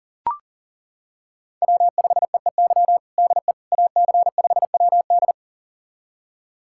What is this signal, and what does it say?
Morse code, heard as a keyed tone, about 40 wpm, 700 Hz. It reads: W5EEY DE AC5WD
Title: USA Callsign DE Callsign - 5 Characters - 40wpm